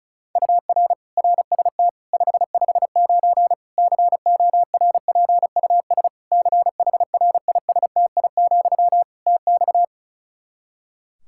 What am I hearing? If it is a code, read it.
Morse code, heard as a keyed tone, about 36 wpm, 700 Hz. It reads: UR RST 559 CORPUS CHRISTI, TX
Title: Practice QSO - Exchange 1 - Beginning - 35wpm